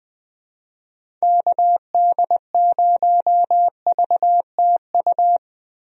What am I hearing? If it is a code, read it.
Morse code, heard as a keyed tone, about 20 wpm, 700 Hz. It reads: KD0VTU
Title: USA Call Signs - 6 Characters - Repeated 3x - 20wpm